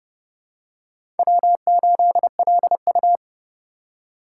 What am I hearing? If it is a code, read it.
Morse code, heard as a keyed tone, about 30 wpm, 700 Hz. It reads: W8LU
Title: USA Call Signs - 4 Characters - 2x Character Spacing - 30wpm